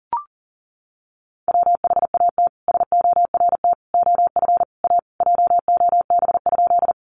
Morse code, heard as a keyed tone, about 40 wpm, 700 Hz. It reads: WHAT SORT OF A JOB?
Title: Sentences from Top 1300 Words 40wpm